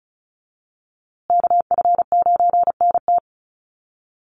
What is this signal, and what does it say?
Morse code, heard as a keyed tone, about 34 wpm, 700 Hz. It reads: KF9NT
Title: USA Call Signs Repeated 2x 35wpm